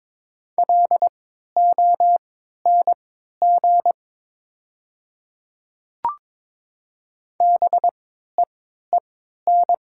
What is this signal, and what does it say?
Morse code, heard as a keyed tone, about 22 wpm, spaced as for 13 wpm, 700 Hz. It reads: LONG BEEN
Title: 3x Character Spacing - Top 200 Words - 22wpm